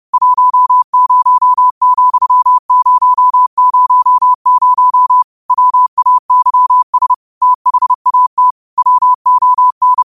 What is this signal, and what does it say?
Morse code, heard as a keyed tone, about 30 wpm, 1000 Hz. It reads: 10,000 WAYS THAT WON
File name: qotdfiles / 30wpm